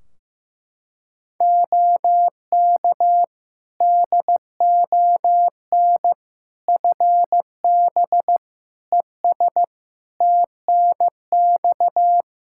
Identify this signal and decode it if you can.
Morse code, heard as a keyed tone, about 15 wpm, 700 Hz. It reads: OK DON FB ES TNX
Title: Practice QSO - Exchange 2 - Info - 15wpm